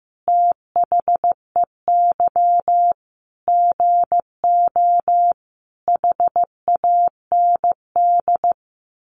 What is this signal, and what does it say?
Morse code, heard as a keyed tone, about 15 wpm, 700 Hz. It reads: THEY GO HAND
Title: Sentences from Top 1200 Words 15wpm